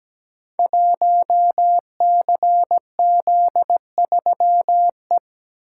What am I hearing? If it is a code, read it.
Morse code, heard as a keyed tone, about 17 wpm, 700 Hz. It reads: 1CZ3E
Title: Code Groups - 5 Letter:Numbers - 17wpm